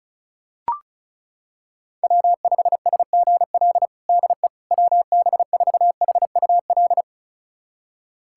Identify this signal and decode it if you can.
Morse code, heard as a keyed tone, about 36 wpm, 700 Hz. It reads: W5SGL DE WB4HUL
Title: USA Callsign DE Callsign 35wpm